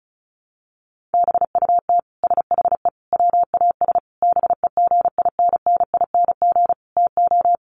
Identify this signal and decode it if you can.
Morse code, heard as a keyed tone, about 34 wpm, 700 Hz. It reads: BUT SHE WAS BEGINNING TO
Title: Sentences from Top 1500 Words 35wpm